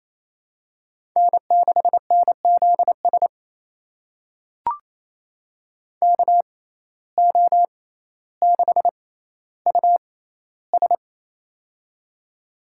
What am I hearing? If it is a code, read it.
Morse code, heard as a keyed tone, about 28 wpm, spaced as for 14 wpm, 700 Hz. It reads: N6NZS KO6US
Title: USA Call Signs - 5 Characters - 6x Character Spacing - 28wpm